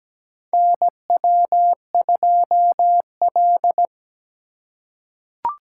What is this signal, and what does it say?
Morse code, heard as a keyed tone, about 17 wpm, 700 Hz. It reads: NW2L
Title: USA Call Signs - 2x Character Spacing - 17wpm